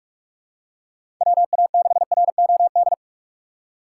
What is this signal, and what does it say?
Morse code, heard as a keyed tone, about 46 wpm, 700 Hz. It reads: WA6ROD
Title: USA Call Signs - 4x Character Spacing - 45wpm